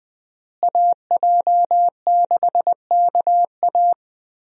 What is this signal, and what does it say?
Morse code, heard as a keyed tone, about 20 wpm, 700 Hz. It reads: AJ6KA
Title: USA Call Signs - 5 Characters - Repeated 2x - 20wpm